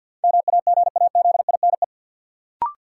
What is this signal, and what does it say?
Morse code, heard as a keyed tone, about 52 wpm, 700 Hz. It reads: MAGAZINE
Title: Sentences from Top 1000 Words - Review Part 7 - New Word Drill - 50wpm